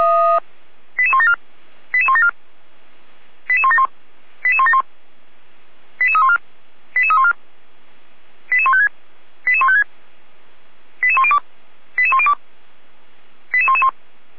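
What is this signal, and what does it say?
Five-tone selective call (ZVEI), heard as quick runs of five tones, tones of about 70 ms. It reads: 88174 88174 88171 88171 88314 88314 88156 88156 88102 88102 88111
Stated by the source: Five-tone ZVEI selective call received by SDR, from the feuerwehr-orlamuende.org fire brigade audio archive.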